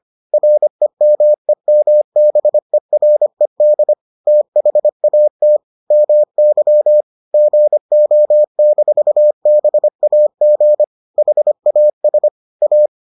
Morse code, heard as a keyed tone, about 25 wpm, 600 Hz. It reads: REMEMBERED THAT MY GO-BAG HAS A